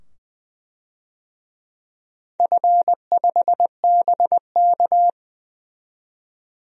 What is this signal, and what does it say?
Morse code, heard as a keyed tone, about 20 wpm, 700 Hz. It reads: F5BK
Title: DXCC-100 Call Signs - 4 Characters - 20wpm